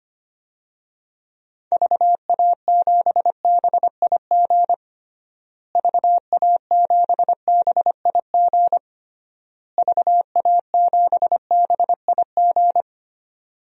Morse code, heard as a keyed tone, about 25 wpm, 700 Hz. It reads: VA7BIG VA7BIG VA7BIG
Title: DXCC-100 Call Signs - 6 Characters - Repeated 3x - 25wpm